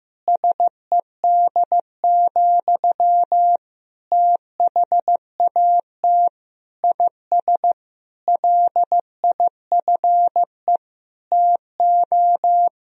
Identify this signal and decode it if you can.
Morse code, heard as a keyed tone, about 15 wpm, 700 Hz. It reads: SED, THAT IS LIFE TO
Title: Sentences from Top 500 Words and Abbreviations 15wpm